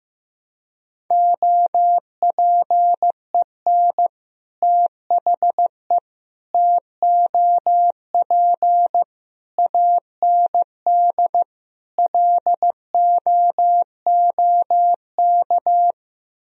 Morse code, heard as a keyed tone, about 16 wpm, 700 Hz. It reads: OPEN THE TOP AND LOOK
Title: Sentences from Top 1000 Words - Review Part 2 - 15wpm